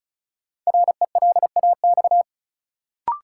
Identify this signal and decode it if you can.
Morse code, heard as a keyed tone, about 36 wpm, 700 Hz. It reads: RELAX
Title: Sentences from Top 1400 Words - New Word Drill Alt - 35wpm